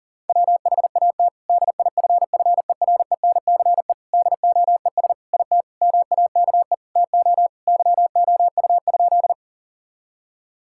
Morse code, heard as a keyed tone, about 40 wpm, 700 Hz. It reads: WHAT DIFFERENCE DOES IT MAKE TO YOU?
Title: Sentences from Top 1000 Words - Review Part 5 - 40wpm